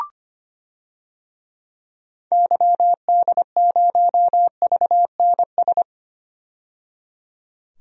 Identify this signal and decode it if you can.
Morse code, heard as a keyed tone, about 25 wpm, 700 Hz. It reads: YD0VNS
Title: DXCC-100 Call Signs - 6 Characters - 25wpm